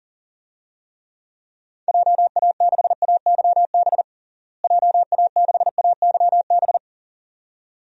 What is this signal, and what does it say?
Morse code, heard as a keyed tone, about 40 wpm, 700 Hz. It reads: JA6AYB JA6AYB
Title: DXCC-100 Call Signs - 6 Characters - Repeated 2x - 40wpm